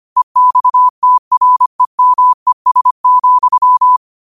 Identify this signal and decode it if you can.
Morse code, heard as a keyed tone, about 25 wpm, 1000 Hz. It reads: EXTREMES,